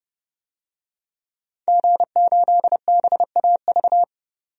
Morse code, heard as a keyed tone, about 30 wpm, 700 Hz. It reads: G8BAV
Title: DXCC-100 Call Signs - 5 Characters - 30wpm